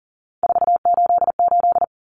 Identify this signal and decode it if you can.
Morse code, heard as a keyed tone, about 40 wpm, 700 Hz. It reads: 488
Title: Common Cut Numbers - 3-Digit 40wpm